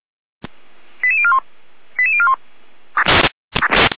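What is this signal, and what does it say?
Five-tone selective call (ZVEI), heard as quick runs of five tones, tones of about 70 ms. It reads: 88041 88041
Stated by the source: Five-tone ZVEI selective call 88333, received by SDR.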